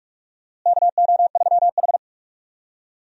Morse code, heard as a keyed tone, about 46 wpm, 700 Hz. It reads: KO3H
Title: USA Call Signs - 4 Characters - Repeated 3x - 45wpm